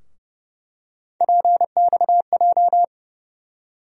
Morse code, heard as a keyed tone, about 30 wpm, 700 Hz. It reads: PXJ
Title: USA Call Sign Postfixes - 3 Letters 30wpm